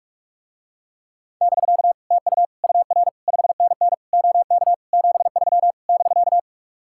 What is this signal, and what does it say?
Morse code, heard as a keyed tone, about 44 wpm, 700 Hz. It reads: <BK> TU UR 5NN OK 73 <BK>